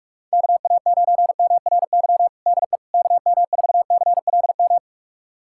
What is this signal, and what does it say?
Morse code, heard as a keyed tone, about 46 wpm, 700 Hz. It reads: KA9MRY DE KM4CLM
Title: USA Callsign DE Callsign - 6 Characters - 45wpm